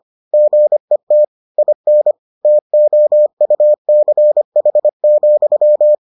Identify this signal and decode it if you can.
Morse code, heard as a keyed tone, about 25 wpm, 600 Hz. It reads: GET IN TOUCH,